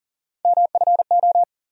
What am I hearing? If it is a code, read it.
Morse code, heard as a keyed tone, about 40 wpm, 700 Hz. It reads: MFO
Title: USA Call Sign Postfixes - 3 Letters 40wpm